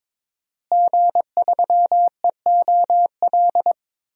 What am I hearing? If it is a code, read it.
Morse code, heard as a keyed tone, about 22 wpm, 700 Hz. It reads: G3EOL
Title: DXCC-100 Call Signs - 5 Characters - Repeated 3x - 22wpm